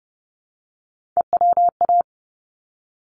Morse code, heard as a keyed tone, about 30 wpm, 700 Hz. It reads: EWA
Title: ARRL Sections 30wpm